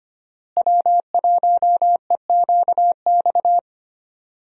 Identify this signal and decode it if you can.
Morse code, heard as a keyed tone, about 25 wpm, 700 Hz. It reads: W1EQX
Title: USA Call Signs - 5 Characters - Repeated 3x - 25wpm